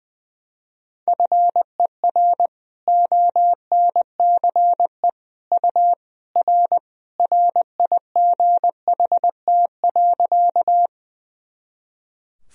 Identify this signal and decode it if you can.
Morse code, heard as a keyed tone, about 21 wpm, 700 Hz. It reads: FER ONCE U R RIGHT.
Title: Sentences from Top 300 Words and Abbreviations 20wpm